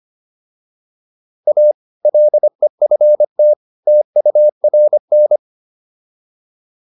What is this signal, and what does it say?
Morse code, heard as a keyed tone, about 25 wpm, 600 Hz. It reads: A LEFT TURN
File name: PR302 25